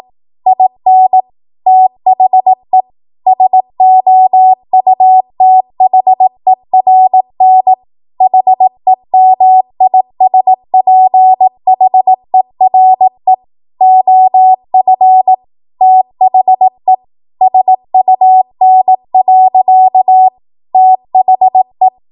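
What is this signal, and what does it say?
Morse code, heard as a keyed tone, about 18 wpm, 750 Hz. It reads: IN THE SOUTHERN HEMISPHERE OF THE SUN. THE
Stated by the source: 18 WPM Morse code practice from a national amateur radio association's code archive.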